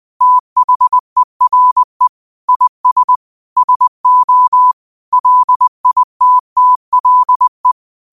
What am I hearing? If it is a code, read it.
Morse code, heard as a keyed tone, about 20 wpm, 1000 Hz. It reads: THERE IS SO LITTLE
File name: qotdfiles / 20wpm